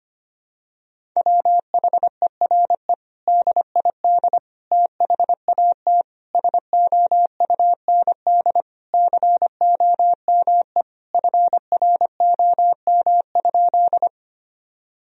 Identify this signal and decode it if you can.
Morse code, heard as a keyed tone, about 25 wpm, 700 Hz. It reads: WHERE DID THAT SOUND COME FROM?
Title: Sentences from Top 1000 Words - Review Part 1 - 25wpm